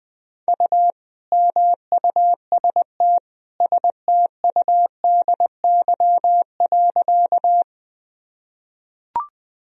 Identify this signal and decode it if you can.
Morse code, heard as a keyed tone, about 20 wpm, 700 Hz. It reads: U MUST STUDY.
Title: Sentences from Top 300 Words and Abbreviations 20wpm